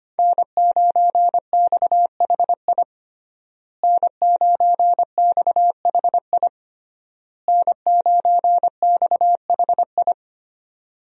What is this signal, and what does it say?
Morse code, heard as a keyed tone, about 25 wpm, 700 Hz. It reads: N9XHI N9XHI N9XHI
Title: USA Call Signs - 5 Characters - Repeated 3x - 25wpm